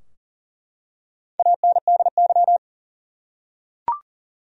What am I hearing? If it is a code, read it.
Morse code, heard as a keyed tone, about 40 wpm, 700 Hz. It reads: ANDY
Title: USA Common Names - Max 4 Letters - 40wpm